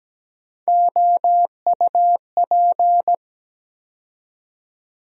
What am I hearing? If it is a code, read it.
Morse code, heard as a keyed tone, about 17 wpm, 700 Hz. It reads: OUP
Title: USA Call Sign Postfixes - 3 Letters 17wpm